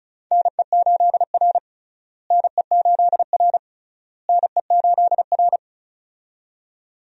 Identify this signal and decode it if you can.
Morse code, heard as a keyed tone, about 36 wpm, 700 Hz. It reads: NE8R NE8R NE8R
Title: USA Call Signs - 4 Characters - Repeated 3x - 35wpm